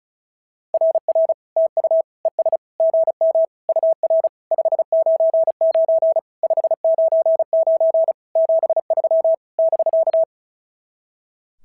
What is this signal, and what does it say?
Morse code, heard as a keyed tone, about 33 wpm, 650 Hz. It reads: RR TU ES GM UR 599 599 73 <BK>
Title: SOTA QSO Practice 35wpm